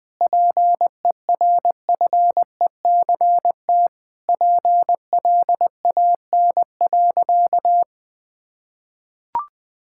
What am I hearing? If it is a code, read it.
Morse code, heard as a keyed tone, about 21 wpm, 700 Hz. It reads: PERFECT PLAN.
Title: Sentences with 7-Word Phrases - 20wpm